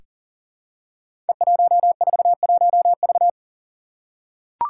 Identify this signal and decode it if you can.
Morse code, heard as a keyed tone, about 40 wpm, 700 Hz. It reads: E141V
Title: Five Challenging Letter-Number Combinations 40wpm